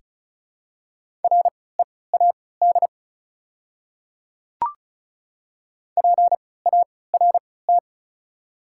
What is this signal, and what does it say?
Morse code, heard as a keyed tone, about 36 wpm, spaced as for 20 wpm, 700 Hz. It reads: READ PART
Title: 3x Character Spacing - Top 200 Words - 35wpm